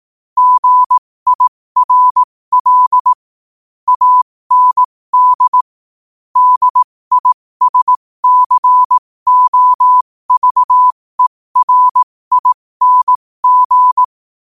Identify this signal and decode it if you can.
Morse code, heard as a keyed tone, about 18 wpm, spaced as for 15 wpm, 1000 Hz. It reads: GIRL AND DISCOVERING